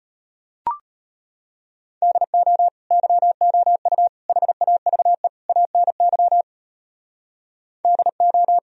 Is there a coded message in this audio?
Yes: Morse code from 2.0 to 8.6 s, heard as a keyed tone, about 32 wpm, 700 Hz. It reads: DO YOU HAVE ANY DO